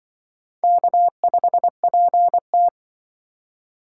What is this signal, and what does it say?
Morse code, heard as a keyed tone, about 24 wpm, 700 Hz. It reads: K5PT
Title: Speed Racing - USA 1x2 Call Signs 20wpm